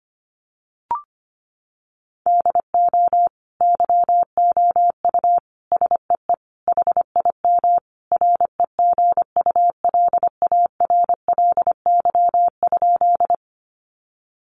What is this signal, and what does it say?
Morse code, heard as a keyed tone, about 25 wpm, 700 Hz. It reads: DO YOU SEE HIM REGULARLY?
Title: Sentences from Top 1400 Words 25wpm